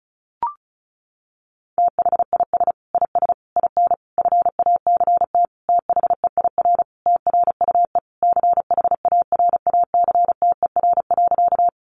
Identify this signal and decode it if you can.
Morse code, heard as a keyed tone, about 34 wpm, 700 Hz. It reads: THIS IS IN FACT THEIR TRUE CHARACTER.
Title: Sentences from Top 1200 Words 35wpm